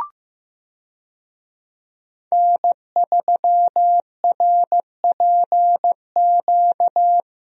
Morse code, heard as a keyed tone, about 15 wpm, 700 Hz. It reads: N3RPQ
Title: USA Call Signs - 5 Characters - Repeated 3x - 15wpm